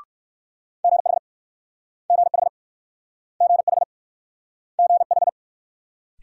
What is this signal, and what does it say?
Morse code, heard as a keyed tone, about 59 wpm, 700 Hz. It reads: GH GH GH GH